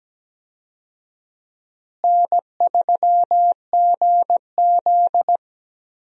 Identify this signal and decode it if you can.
Morse code, heard as a keyed tone, about 17 wpm, 700 Hz. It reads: N3GZ
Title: USA Call Signs - 4 Characters - Repeated 3x - 17wpm